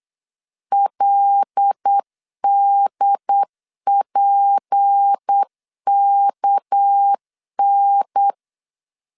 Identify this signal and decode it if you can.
Morse code, heard as a keyed tone, about 8 wpm, 800 Hz. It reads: LDPKN
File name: LEZIONI / 6 PKX0 / 6 PKX 8wpm 3041